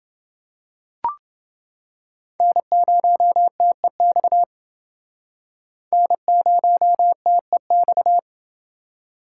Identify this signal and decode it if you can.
Morse code, heard as a keyed tone, about 29 wpm, 700 Hz. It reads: N0TEX N0TEX